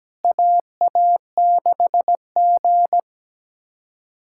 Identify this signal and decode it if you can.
Morse code, heard as a keyed tone, about 17 wpm, 700 Hz. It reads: AA6G
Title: USA Callsign DE Callsign - 4 Characters - 17wpm